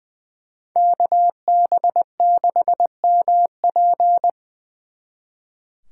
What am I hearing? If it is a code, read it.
Morse code, heard as a keyed tone, about 24 wpm, 700 Hz. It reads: KB6MP
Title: Speed Racing - USA 2x2 Call Signs 20wpm